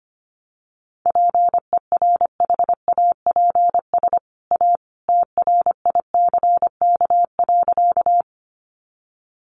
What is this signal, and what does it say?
Morse code, heard as a keyed tone, about 25 wpm, 700 Hz. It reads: PERHAPS A TRICK.
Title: Sentences from Top 1500 Words 25wpm